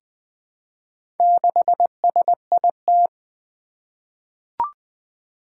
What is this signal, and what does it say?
Morse code, heard as a keyed tone, about 20 wpm, 700 Hz. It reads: 6SIT
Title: Four Challenging Letter-Number Combinations 20wpm